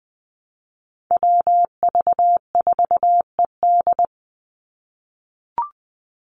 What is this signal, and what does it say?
Morse code, heard as a keyed tone, about 20 wpm, 700 Hz. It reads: WV4ED
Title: USA Call Signs 20wpm